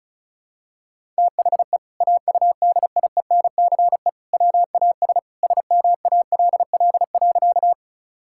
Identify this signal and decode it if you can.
Morse code, heard as a keyed tone, about 36 wpm, 700 Hz. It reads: THE AUDIENCE WAS SMALL.
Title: Sentences from Top 1000 Words - Review Part 5 - 35wpm